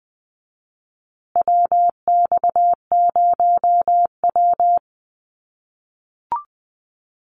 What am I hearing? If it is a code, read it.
Morse code, heard as a keyed tone, about 20 wpm, 700 Hz. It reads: WX0W
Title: USA Call Signs Repeated 2x 20wpm